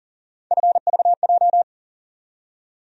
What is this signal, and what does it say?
Morse code, heard as a keyed tone, about 40 wpm, 700 Hz. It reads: FVJ
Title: USA Call Sign Postfixes - 3 Letters 40wpm